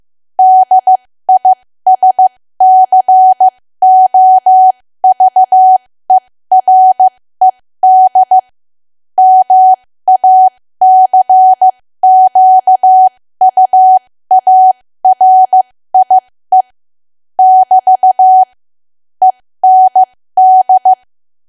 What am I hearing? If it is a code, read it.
Morse code, heard as a keyed tone, about 15 wpm, spaced as for 13 wpm, 750 Hz. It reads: DISCOVERED MACQUARIE <BT> END
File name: cwa / W1AW-Level2 / 13WPM